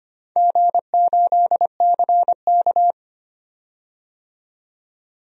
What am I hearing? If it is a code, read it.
Morse code, heard as a keyed tone, about 25 wpm, 700 Hz. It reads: G8CK
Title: DXCC-100 Call Signs - 4 Characters - 25wpm